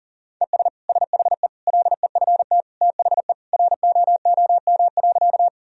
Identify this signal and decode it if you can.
Morse code, heard as a keyed tone, about 40 wpm, 700 Hz. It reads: ES SHE LEFT THE ROOM.
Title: Sentences from Top 300 Words and Abbreviations 40wpm